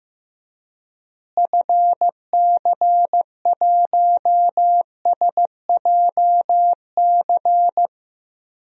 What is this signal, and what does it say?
Morse code, heard as a keyed tone, about 15 wpm, 700 Hz. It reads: FC1SJC
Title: DXCC-100 Call Signs - 6 Characters - Repeated 3x - 15wpm